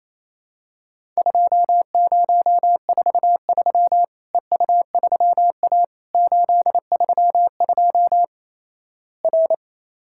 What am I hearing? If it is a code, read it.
Morse code, heard as a keyed tone, about 28 wpm, 700 Hz. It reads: 2043 EU3A 832 R
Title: Worked All Europe DX Contest - QTC Exchange - 28wpm